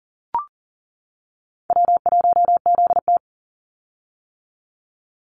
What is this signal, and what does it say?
Morse code, heard as a keyed tone, about 40 wpm, 700 Hz. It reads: W1ZT
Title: Easy Call Signs - 40wpm